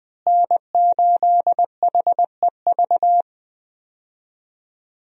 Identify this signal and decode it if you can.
Morse code, heard as a keyed tone, about 20 wpm, 700 Hz. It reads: N8HEV
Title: USA Call Signs - 5 Characters - Repeated 3x - 20wpm